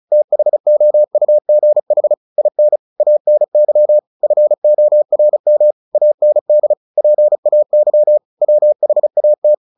Morse code, heard as a keyed tone, about 34 wpm, 600 Hz. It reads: THOUGH IN ANY FORM AND PAY WHAT
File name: the old man and the sea 2 35wpm